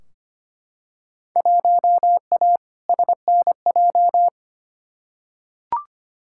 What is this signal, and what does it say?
Morse code, heard as a keyed tone, about 25 wpm, 700 Hz. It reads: 1A SNJ
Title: Field Day Exchanges - 25wpm